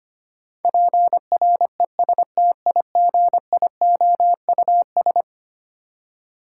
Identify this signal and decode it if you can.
Morse code, heard as a keyed tone, about 25 wpm, 700 Hz. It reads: PRESTIGIOUS
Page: morse-code-ninja-practice-25wpm